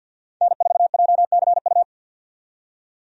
Morse code, heard as a keyed tone, about 52 wpm, 700 Hz. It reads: N4JCU
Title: USA Call Signs - 5 Characters - 50wpm